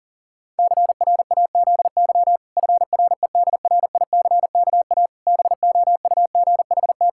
Morse code, heard as a keyed tone, about 40 wpm, 700 Hz. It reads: CRAZY FREDRICKA BOUGHT
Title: Pangram Challenge v2 - 40wpm